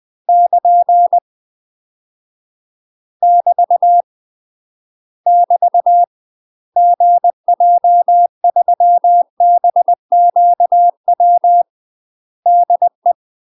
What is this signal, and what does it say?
Morse code, heard as a keyed tone, about 20 wpm, 700 Hz. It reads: <KN> <BT> <BT> GJ3BQW DE
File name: UK MASTER QSOs TRAINING / 20WPM / UK MASTER QSOs 0025